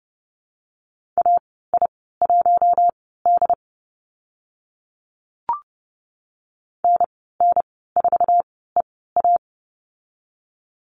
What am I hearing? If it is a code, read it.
Morse code, heard as a keyed tone, about 30 wpm, spaced as for 17 wpm, 700 Hz. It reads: AI1D NN4EA
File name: Lesson 14g - Number 2 - Call Signs 30wpm